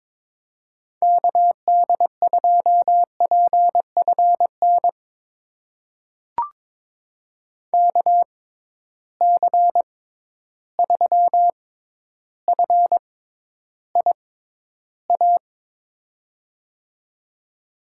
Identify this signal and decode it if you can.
Morse code, heard as a keyed tone, about 22 wpm, spaced as for 11 wpm, 700 Hz. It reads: KD2PFN KC3FIA
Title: USA Call Signs - 6 Characters - 6x Character Spacing - 22wpm